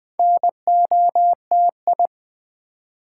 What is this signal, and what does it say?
Morse code, heard as a keyed tone, about 20 wpm, 700 Hz. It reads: NOTI
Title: Build A Word - 2 Letters At A Time - 6-Letter Words - 20wpm